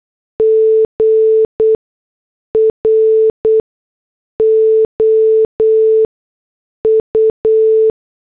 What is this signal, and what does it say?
Morse code, heard as a keyed tone, about 8 wpm, spaced as for 6 wpm, 440 Hz. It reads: GROU